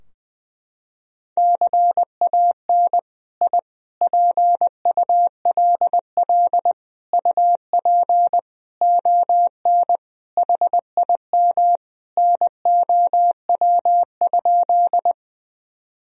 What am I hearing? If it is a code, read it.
Morse code, heard as a keyed tone, about 20 wpm, 700 Hz. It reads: CAN I PULL UP ON HIM NOW?
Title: Sentences from Top 1000 Words - Review Part 2 - 20wpm